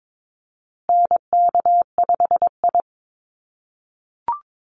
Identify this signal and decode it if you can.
Morse code, heard as a keyed tone, about 22 wpm, 700 Hz. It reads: NK5I
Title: USA Call Signs Repeated 2x 22wpm